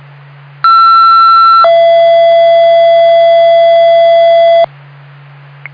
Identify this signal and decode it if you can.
Two-tone page, heard as a one-second tone, then a three-second tone, about 1340 then 670 Hz, with noise. It is heard from 0.6 to 4.6 s.